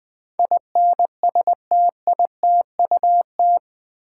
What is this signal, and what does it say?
Morse code, heard as a keyed tone, about 20 wpm, 700 Hz. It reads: INSTITUT
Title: Build A Word - 4 Letters At A Time - 11-Letter Words - 20wpm